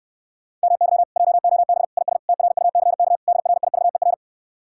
Morse code, heard as a keyed tone, about 67 wpm, 700 Hz. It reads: MY JOB IS TRUCK DRIVER